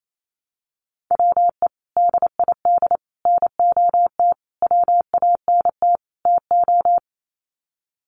Morse code, heard as a keyed tone, about 28 wpm, 700 Hz. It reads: WE DID NOT WANT TO
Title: Sets of 5 Words Encore 28wpm